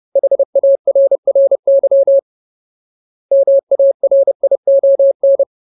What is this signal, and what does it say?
Morse code, heard as a keyed tone, about 30 wpm, 550 Hz. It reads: HARRY MARION
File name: Ultimate Word List / 30WPM Ultimate